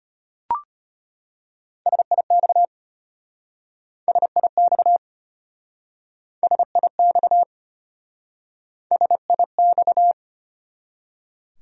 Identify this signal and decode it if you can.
Morse code, heard as a keyed tone, about 33 wpm, 700 Hz. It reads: SIX SIX SIX SIX